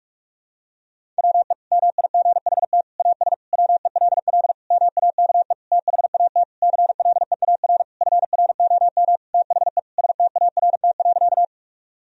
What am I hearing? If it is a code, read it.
Morse code, heard as a keyed tone, about 41 wpm, 700 Hz. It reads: WE MIGHT AS WELL MAKE THAT CLEAR FROM THE START.
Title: Sentences from Top 1000 Words - Review Part 1 - 45wpm